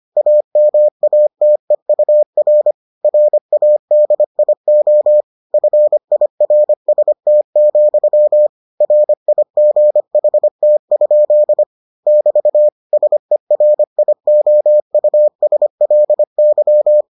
Morse code, heard as a keyed tone, about 25 wpm, 600 Hz. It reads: AMATEUR RADIO FIRST, RIGHT? <BT> SERIOUSLY